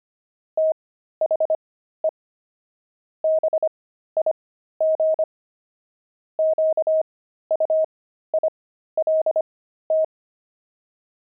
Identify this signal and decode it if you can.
Morse code, heard as a keyed tone, about 25 wpm, spaced as for 13 wpm, 630 Hz. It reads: THE BIG QUILT
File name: pangram-fs-13-cs-25-jinxed